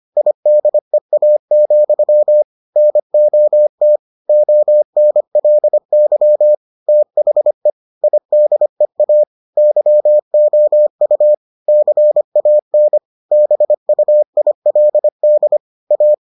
Morse code, heard as a keyed tone, about 25 wpm, 600 Hz. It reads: IDEA, NOT ONLY THE IDEA YOU CAN BUILD A